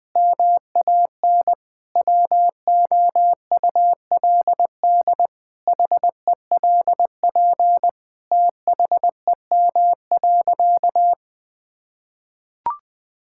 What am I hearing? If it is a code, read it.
Morse code, heard as a keyed tone, about 20 wpm, 700 Hz. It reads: MAN WOULD HELP THEM.
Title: Sentences from Top 1000 Words - Review Part 5 - 20wpm